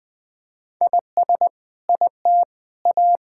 Morse code, heard as a keyed tone, about 20 wpm, 700 Hz. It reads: IS IT A
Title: Sentences from Top 1000 Words - Review Part 6 - 20wpm